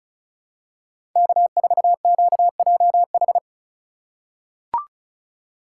Morse code, heard as a keyed tone, about 34 wpm, 700 Hz. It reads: K4QJH
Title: USA Call Signs - 5 Characters - 4x Character Spacing - 35wpm